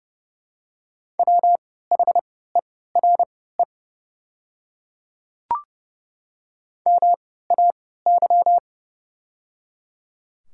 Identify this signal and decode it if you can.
Morse code, heard as a keyed tone, about 30 wpm, spaced as for 17 wpm, 700 Hz. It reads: WHERE MAY
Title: 3x Character Spacing - Top 200 Words - 30wpm